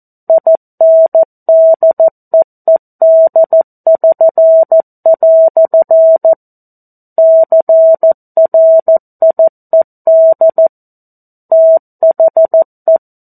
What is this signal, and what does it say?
Morse code, heard as a keyed tone, about 14 wpm, 650 Hz. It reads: INDEED<SN>" CRIED THE
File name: 02-Chapter-02 14 WPM